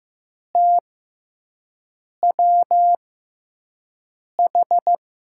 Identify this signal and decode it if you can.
Morse code, heard as a keyed tone, about 15 wpm, spaced as for 5 wpm, 700 Hz. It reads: TWH
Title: USA Call Signs - 6x Character Spacing - 15wpm